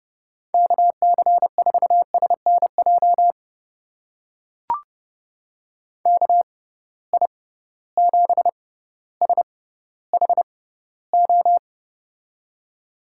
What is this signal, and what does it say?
Morse code, heard as a keyed tone, about 30 wpm, spaced as for 15 wpm, 700 Hz. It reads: KC4SNJ KI7SHO
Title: USA Call Signs - 6 Characters - 6x Character Spacing - 30wpm